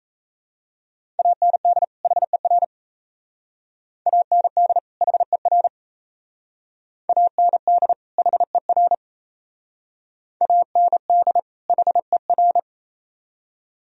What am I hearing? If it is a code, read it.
Morse code, heard as a keyed tone, about 36 wpm, 700 Hz. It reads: AND HER AND HER AND HER AND HER